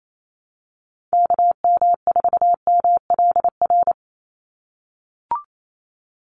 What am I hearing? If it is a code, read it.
Morse code, heard as a keyed tone, about 28 wpm, 700 Hz. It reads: KM4MLR
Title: USA Call Signs 28wpm